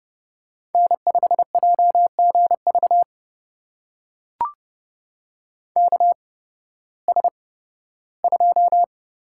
Morse code, heard as a keyed tone, about 30 wpm, spaced as for 16 wpm, 700 Hz. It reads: N5JGV KS2
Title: USA Call Signs - 8x Character Spacing - 30wpm